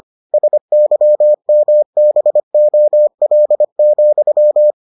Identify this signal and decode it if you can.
Morse code, heard as a keyed tone, about 25 wpm, 600 Hz. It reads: SYMBOL,